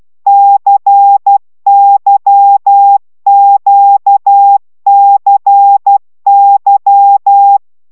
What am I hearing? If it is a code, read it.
Morse code, heard as a keyed tone, about 12 wpm, 800 Hz. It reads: CYQCY
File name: taviro tanulas / cw 75 15 10